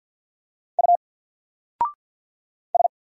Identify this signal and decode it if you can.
Morse code, heard as a keyed tone, about 50 wpm, 700 Hz. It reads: U S
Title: Single Letter-Number 50wpm